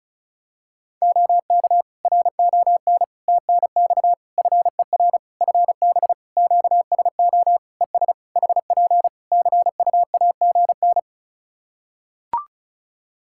Morse code, heard as a keyed tone, about 36 wpm, 700 Hz. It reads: OK RON TNX FER FB QSO ES HP CUAGN
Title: Practice QSO - Exchange 4 - Ending - 35wpm